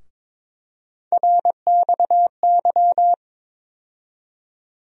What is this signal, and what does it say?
Morse code, heard as a keyed tone, about 22 wpm, 700 Hz. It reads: RXY
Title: USA Call Sign Postfixes - 3 Letters 22wpm